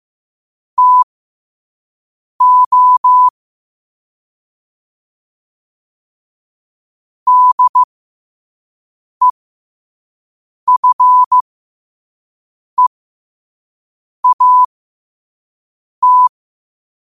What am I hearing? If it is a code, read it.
Morse code, heard as a keyed tone, about 15 wpm, spaced as for 5 wpm, 1000 Hz. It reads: TO DEFEAT
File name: qotdfiles / 5wpm